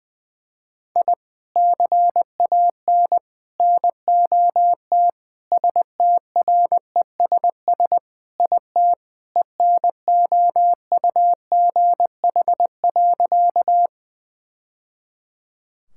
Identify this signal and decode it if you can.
Morse code, heard as a keyed tone, about 21 wpm, 700 Hz. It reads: I CAN NOT STRESS IT ENOUGH.
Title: Sentences from Top 1000 Words - Review Part 6 - 20wpm